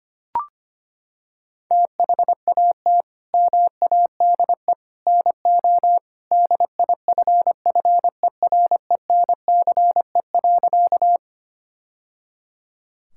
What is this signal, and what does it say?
Morse code, heard as a keyed tone, about 25 wpm, 700 Hz. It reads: THAT MADE NO DIFFERENCE.
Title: Sentences from Top 1000 Words - Review Part 5 - 25wpm